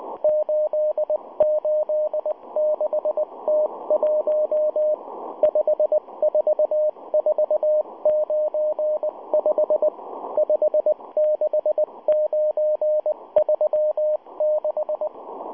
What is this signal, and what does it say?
Morse code, heard as a keyed tone, about 20 wpm, 610 Hz. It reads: 886T1 54495 56936